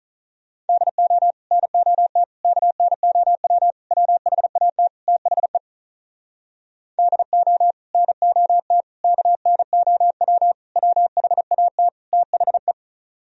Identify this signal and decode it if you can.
Morse code, heard as a keyed tone, about 46 wpm, 700 Hz. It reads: DO NOT KNOW WHAT THE DO NOT KNOW WHAT THE